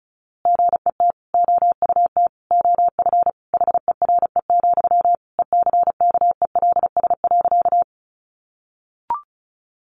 Morse code, heard as a keyed tone, about 34 wpm, 700 Hz. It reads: GET OUT OF HERE, ECKELS.
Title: A Sound Of Thunder 35wpm